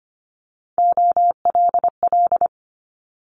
Morse code, heard as a keyed tone, about 25 wpm, 700 Hz. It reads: OLL
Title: Multi-letter Phonemes Quick Fire 25wpm